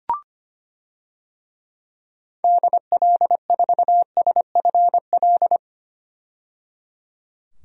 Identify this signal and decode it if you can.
Morse code, heard as a keyed tone, about 25 wpm, 690 Hz. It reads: DL4SFL
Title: DXCC-100 Call Signs - 6 Characters - 25wpm